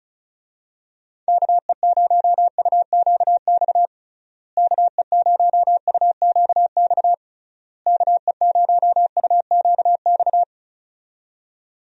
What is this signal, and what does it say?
Morse code, heard as a keyed tone, about 36 wpm, 700 Hz. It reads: KE0UQX KE0UQX KE0UQX
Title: USA Call Signs - 6 Characters - Repeated 3x - 35wpm